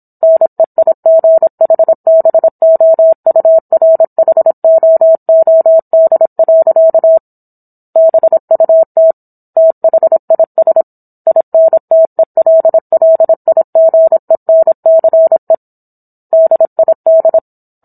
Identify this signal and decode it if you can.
Morse code, heard as a keyed tone, about 27 wpm, 650 Hz. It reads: NEIGHBOURHOOD. BUT THIS INTELLIGENCE DID